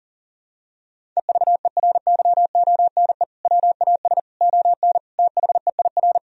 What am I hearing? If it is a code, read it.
Morse code, heard as a keyed tone, about 40 wpm, 700 Hz. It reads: EVERYONE WAS ON THEIR
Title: Sentences from Top 1000 Words - Review Part 1 - 40wpm